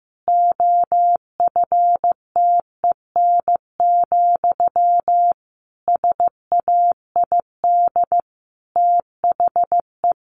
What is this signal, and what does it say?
Morse code, heard as a keyed tone, about 15 wpm, 700 Hz. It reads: OFTEN, SAID THE
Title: Sentences from Top 1200 Words 15wpm